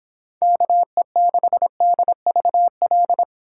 Morse code, heard as a keyed tone, about 30 wpm, 700 Hz. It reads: KE6DVL